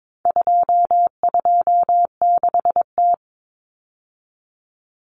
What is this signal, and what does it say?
Morse code, heard as a keyed tone, about 22 wpm, 700 Hz. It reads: 226T
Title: Common Cut Numbers - 4-Digit 22wpm